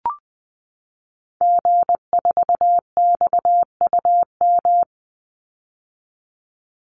Morse code, heard as a keyed tone, about 20 wpm, 700 Hz. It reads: G4XUM
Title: Call Signs - 20wpm